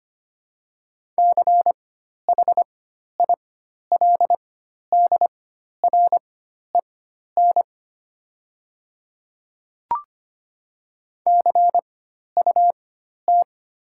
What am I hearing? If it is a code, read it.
Morse code, heard as a keyed tone, about 25 wpm, spaced as for 12 wpm, 700 Hz. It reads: CHILDREN CUT
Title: 4x Character Spacing - Top 300 Words - 25wpm